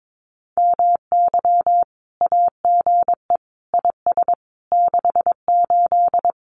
Morse code, heard as a keyed tone, about 22 wpm, 700 Hz. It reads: MY AGE IS 68
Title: Common QSO Phrases Encore 22wpm